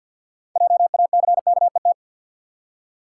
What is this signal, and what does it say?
Morse code, heard as a keyed tone, about 50 wpm, 700 Hz. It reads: JACKET
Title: Six Letter Words - 50wpm